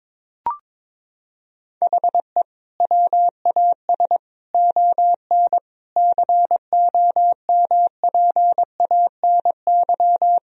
Morse code, heard as a keyed tone, about 22 wpm, 700 Hz. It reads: HE WAS ON COMPANY
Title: Sentences from Top 1000 Words - Review Part 3 - 22wpm